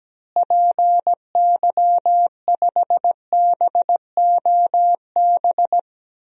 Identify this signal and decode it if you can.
Morse code, heard as a keyed tone, about 17 wpm, 700 Hz. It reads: PY5BOB
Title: DXCC-100 Call Signs - 6 Characters - Repeated 2x - 17wpm